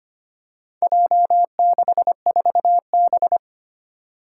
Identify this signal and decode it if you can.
Morse code, heard as a keyed tone, about 25 wpm, 700 Hz. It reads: J64B
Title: Four Challenging Letter-Number Combinations 25wpm